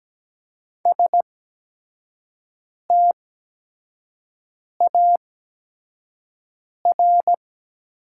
Morse code, heard as a keyed tone, about 17 wpm, spaced as for 5 wpm, 700 Hz. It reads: STAR
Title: 8x Character Spacing - Top 500 Words - 17wpm